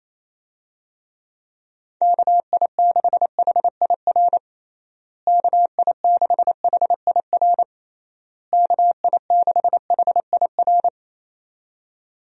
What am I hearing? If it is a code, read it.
Morse code, heard as a keyed tone, about 28 wpm, 700 Hz. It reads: KI6HIR KI6HIR KI6HIR
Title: USA Call Signs - 6 Characters - Repeated 3x - 28wpm